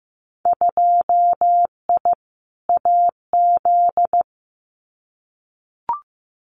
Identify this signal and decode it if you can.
Morse code, heard as a keyed tone, about 15 wpm, 700 Hz. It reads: 2I AZ
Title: Winter Field Day Exchanges 15wpm